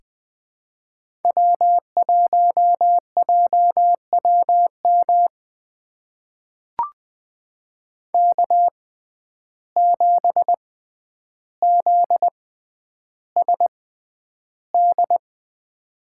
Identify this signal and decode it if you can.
Morse code, heard as a keyed tone, about 20 wpm, spaced as for 10 wpm, 700 Hz. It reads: W1JWM K7ZSD
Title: USA Call Signs - 6x Character Spacing - 20wpm